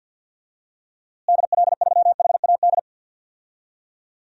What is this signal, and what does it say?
Morse code, heard as a keyed tone, about 52 wpm, 700 Hz. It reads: DL3HAD
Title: DXCC-100 Call Signs - 6 Characters - Repeated 2x - 50wpm